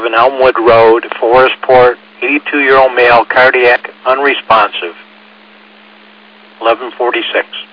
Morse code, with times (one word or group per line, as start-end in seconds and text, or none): none